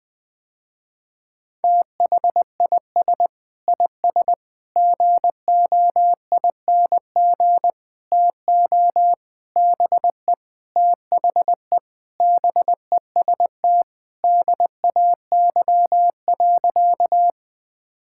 Morse code, heard as a keyed tone, about 21 wpm, 700 Hz. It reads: THIS IS GOING TO BE THE BEST DAY.
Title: Sentences with 4-Word Phrases - Spoken Beginning - 20wpm